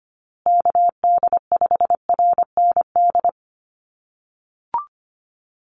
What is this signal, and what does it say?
Morse code, heard as a keyed tone, about 25 wpm, 700 Hz. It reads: KD5RND
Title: USA Call Signs Repeated 2x 25wpm